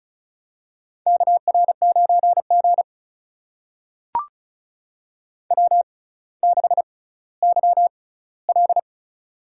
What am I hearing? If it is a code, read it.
Morse code, heard as a keyed tone, about 36 wpm, spaced as for 18 wpm, 700 Hz. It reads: KR9G W6YL
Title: USA Call Signs - 4 Characters - 6x Character Spacing - 35wpm